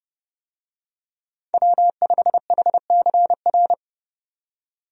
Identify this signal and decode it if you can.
Morse code, heard as a keyed tone, about 30 wpm, 700 Hz. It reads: W5HCR
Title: USA Call Signs - 5 Characters - 30wpm